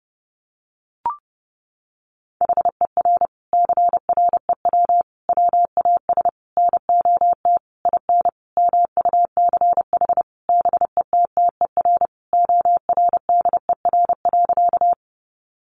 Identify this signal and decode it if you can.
Morse code, heard as a keyed tone, about 30 wpm, 700 Hz. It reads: HER CREW WAS NOT IN MUCH BETTER ORDER.
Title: Sentences from Top 1500 Words 30wpm